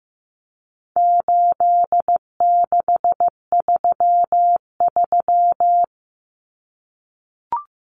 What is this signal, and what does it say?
Morse code, heard as a keyed tone, about 15 wpm, 700 Hz. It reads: 8633
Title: Numbers - 4-Digit 15wpm